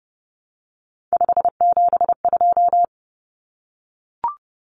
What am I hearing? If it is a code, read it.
Morse code, heard as a keyed tone, about 30 wpm, 700 Hz. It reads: 572
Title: Common Cut Numbers - 3-Digit 30wpm